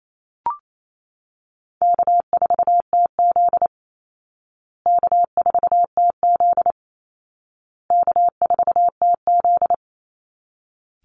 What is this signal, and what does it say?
Morse code, heard as a keyed tone, about 28 wpm, 700 Hz. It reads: K4TZ K4TZ K4TZ
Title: Easy Call Signs Repeated 3x - 28wpm